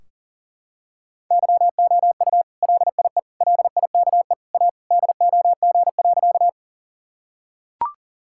Morse code, heard as a keyed tone, about 40 wpm, 700 Hz. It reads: YOU LIE LIKE A DOG.
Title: Sentences from Top 1600 Words 40wpm